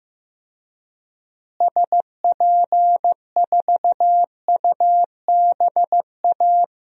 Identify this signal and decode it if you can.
Morse code, heard as a keyed tone, about 15 wpm, 700 Hz. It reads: SP4UBA
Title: DXCC-100 Call Signs - 6 Characters - Repeated 2x - 15wpm